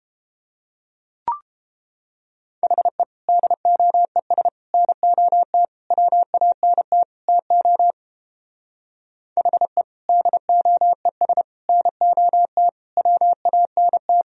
Morse code, heard as a keyed tone, about 29 wpm, 700 Hz. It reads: HE DOES NOT WANT TO HE DOES NOT WANT